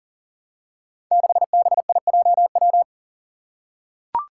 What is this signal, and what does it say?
Morse code, heard as a keyed tone, about 39 wpm, 700 Hz. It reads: 6BIJW
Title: Five Challenging Letter-Number Combinations 40wpm